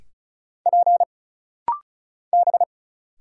Morse code, heard as a keyed tone, about 36 wpm, 700 Hz. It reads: P B
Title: Single Letters - Rapid-Fire 35wpm